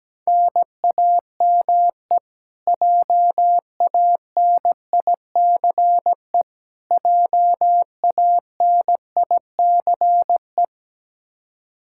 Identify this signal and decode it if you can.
Morse code, heard as a keyed tone, about 17 wpm, 700 Hz. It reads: NAME JANICE JANICE
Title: Practice QSO - Exchange 1 - Beginning - 17wpm